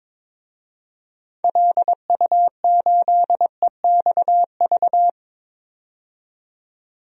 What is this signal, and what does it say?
Morse code, heard as a keyed tone, about 22 wpm, 700 Hz. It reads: LU8EXV
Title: DXCC-100 Call Signs - 6 Characters - 22wpm